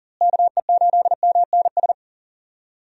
Morse code, heard as a keyed tone, about 40 wpm, 700 Hz. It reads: KE8MNS
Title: USA Call Signs - 6 Characters - Repeated 3x - 40wpm